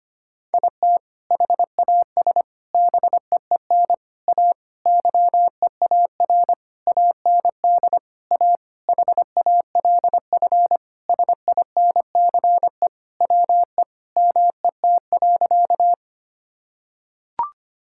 Morse code, heard as a keyed tone, about 25 wpm, 700 Hz. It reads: IT HAS BEEN A YEAR AND A HALF SINCE WE MET.
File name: practice / 25wpm / Sentences with 4-Word Phrases - 25wpm